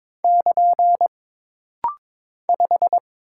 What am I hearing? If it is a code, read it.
Morse code, heard as a keyed tone, about 22 wpm, 700 Hz. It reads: <KN> 5
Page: This rapid-fire timing is geared to push your ICR (Instant Character Recognition) skills.